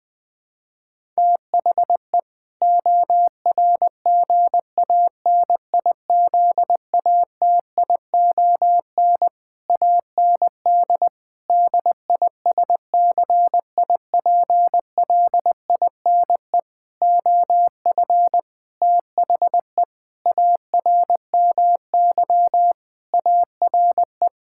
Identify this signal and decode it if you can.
Morse code, heard as a keyed tone, about 20 wpm, 700 Hz. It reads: THE ORGANIZATION AND DISCIPLINE OF THE ARMY ARE
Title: Sentences from Top 1000 Words - Review Part 6 - 20wpm